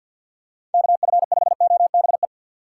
Morse code, heard as a keyed tone, about 49 wpm, 700 Hz. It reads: KF5OBE
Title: USA Call Signs - 6 Characters - 3x Character Spacing - 50wpm